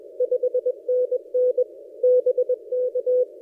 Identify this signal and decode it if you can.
Morse code, heard as a keyed tone, about 21 wpm, 500 Hz. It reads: 5NN BK
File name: posts / a35jr-12cw- -2336z